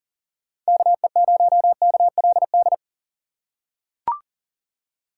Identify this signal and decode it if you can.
Morse code, heard as a keyed tone, about 40 wpm, 700 Hz. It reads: KE0KLD
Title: USA Call Signs - 6 Characters - 3x Character Spacing - 40wpm